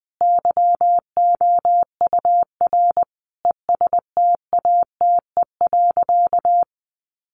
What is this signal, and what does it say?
Morse code, heard as a keyed tone, about 20 wpm, 700 Hz. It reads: YOUR ESTATE.
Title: Sentences from Top 1200 Words 20wpm